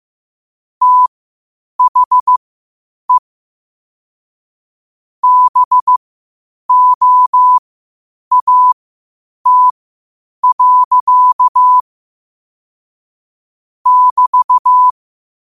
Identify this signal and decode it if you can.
Morse code, heard as a keyed tone, about 15 wpm, spaced as for 8 wpm, 1000 Hz. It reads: THE BOAT. <BT>
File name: 8wpm